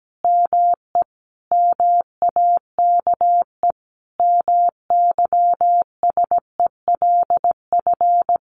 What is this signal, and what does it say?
Morse code, heard as a keyed tone, about 17 wpm, 700 Hz. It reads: ME MAKE MYSELF
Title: Sentences from Top 1300 Words 17wpm